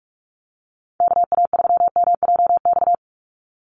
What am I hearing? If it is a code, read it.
Morse code, heard as a keyed tone, about 44 wpm, 700 Hz. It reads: KA3MJX
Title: USA Call Signs 45wpm